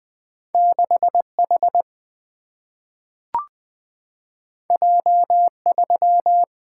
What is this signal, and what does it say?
Morse code, heard as a keyed tone, about 20 wpm, 700 Hz. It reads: 6 H J 3
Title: Single-Letter Single-Number 20wpm